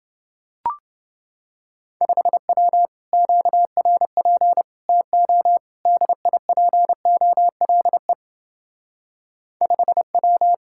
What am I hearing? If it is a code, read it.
Morse code, heard as a keyed tone, about 29 wpm, 700 Hz. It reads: 5W QRP TO DIPOLE 5W